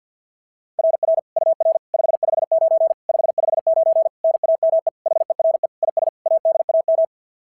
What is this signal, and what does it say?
Morse code, heard as a keyed tone, about 53 wpm, 690 Hz. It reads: RR UR 559 559 NAME HERE IS ADAM
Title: SOTA QSO Practice 50wpm